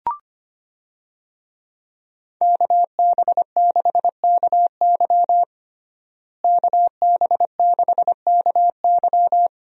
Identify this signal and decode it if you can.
Morse code, heard as a keyed tone, about 25 wpm, 700 Hz. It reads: KB6KY KB6KY
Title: USA Call Signs - 5 Characters - Repeated 3x - 25wpm